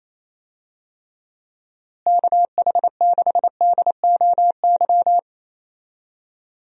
Morse code, heard as a keyed tone, about 28 wpm, 700 Hz. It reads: KH6DOY
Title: DXCC-100 Call Signs - 6 Characters - Repeated 2x - 28wpm